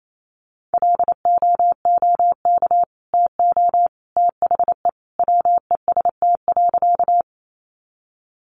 Morse code, heard as a keyed tone, about 28 wpm, 700 Hz. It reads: LOOK TO THE WEST.
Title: Sentences from Top 400 Words 28wpm